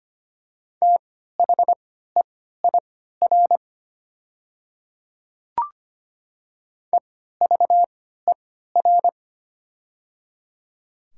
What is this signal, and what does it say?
Morse code, heard as a keyed tone, about 25 wpm, spaced as for 14 wpm, 700 Hz. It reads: THEIR EVER
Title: 3x Character Spacing - Top 500 Words - 25wpm